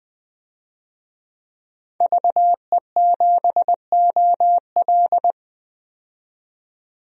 Morse code, heard as a keyed tone, about 20 wpm, 700 Hz. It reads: VE7OL
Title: DXCC-100 Call Signs - 5 Characters - 20wpm